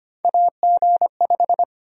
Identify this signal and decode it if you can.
Morse code, heard as a keyed tone, about 25 wpm, 700 Hz. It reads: AG5
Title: USA Call Sign Prefixes - 25wpm